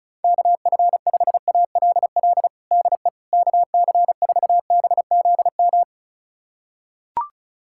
Morse code, heard as a keyed tone, about 36 wpm, 700 Hz. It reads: KF5ALL DE KC4BZM
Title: USA Callsign DE Callsign - 6 Characters - 35wpm